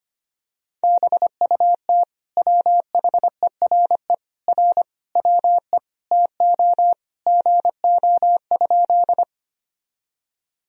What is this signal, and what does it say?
Morse code, heard as a keyed tone, about 25 wpm, 700 Hz. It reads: BUT WHERE R WE TO GO?
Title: Sentences from Top 200 Words and Abbreviations 25wpm